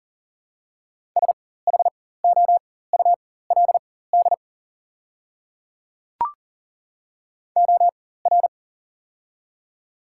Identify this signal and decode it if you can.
Morse code, heard as a keyed tone, about 40 wpm, spaced as for 19 wpm, 700 Hz. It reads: SHOULD OR
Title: 4x Character Spacing - Top 300 Words - 40wpm